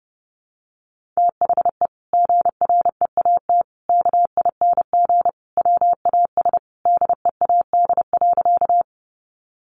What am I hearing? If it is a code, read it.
Morse code, heard as a keyed tone, about 30 wpm, 700 Hz. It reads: THE GREAT KING WAS DEAD.
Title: Sentences from Top 1500 Words 30wpm